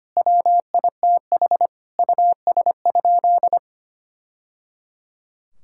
Morse code, heard as a keyed tone, about 25 wpm, 700 Hz. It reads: WITH US?
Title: Sentences from Top 1000 Words - Review Part 6 - 25wpm